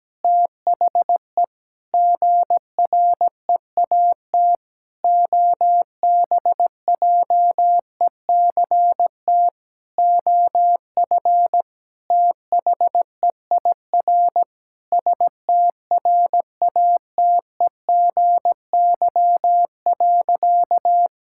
Morse code, heard as a keyed tone, about 17 wpm, 700 Hz. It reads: THE GREAT OBJECT OF THEIR STRATEGY.
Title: Sentences from Top 1000 Words - Review Part 5 - 17wpm